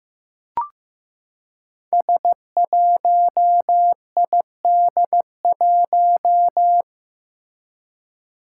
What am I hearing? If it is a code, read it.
Morse code, heard as a keyed tone, about 15 wpm, 700 Hz. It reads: S1ID1
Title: Five Challenging Letter-Number Combinations 15wpm